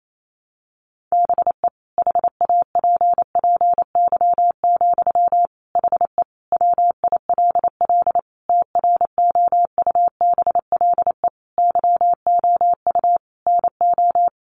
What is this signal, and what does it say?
Morse code, heard as a keyed tone, about 28 wpm, 700 Hz. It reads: BE HAPPY, HE WILL TROUBLE YOU NO
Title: Sentences from Top 1200 Words 28wpm